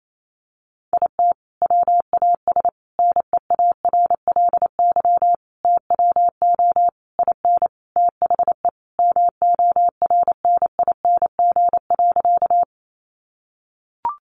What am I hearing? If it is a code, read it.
Morse code, heard as a keyed tone, about 28 wpm, 700 Hz. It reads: IT WAS NEARLY TWO IN THE MORNING.
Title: Sentences from Top 1300 Words 28wpm